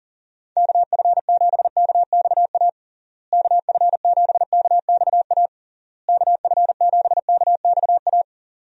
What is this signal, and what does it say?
Morse code, heard as a keyed tone, about 40 wpm, 700 Hz. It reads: KF7KXA KF7KXA KF7KXA
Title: USA Call Signs - 6 Characters - Repeated 3x - 40wpm